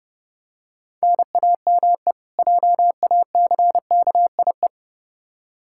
Morse code, heard as a keyed tone, about 30 wpm, 700 Hz. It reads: NAME JACKIE
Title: Practice QSO - Exchange 1 - Beginning - 30wpm